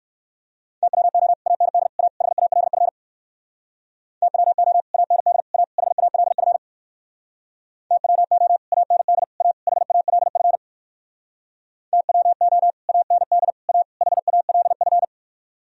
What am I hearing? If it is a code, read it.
Morse code, heard as a keyed tone, about 67 wpm, 700 Hz. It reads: TWO AND A HALF TWO AND A HALF TWO AND A HALF TWO AND A HALF